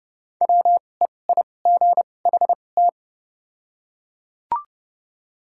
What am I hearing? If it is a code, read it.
Morse code, heard as a keyed tone, about 30 wpm, spaced as for 22 wpm, 690 Hz. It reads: WEIGHT
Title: 2x Character Spacing - Top 500 Words - 30wpm